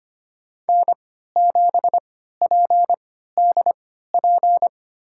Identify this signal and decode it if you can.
Morse code, heard as a keyed tone, about 25 wpm, spaced as for 14 wpm, 700 Hz. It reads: N7PDP
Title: USA Call Signs - 5 Characters - 3x Character Spacing - 25wpm